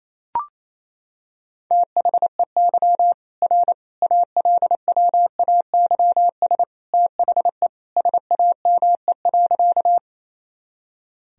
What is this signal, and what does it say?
Morse code, heard as a keyed tone, about 28 wpm, 700 Hz. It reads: THEY R ALWAYS THE SAME.
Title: Sentences from Top 300 Words and Abbreviations 28wpm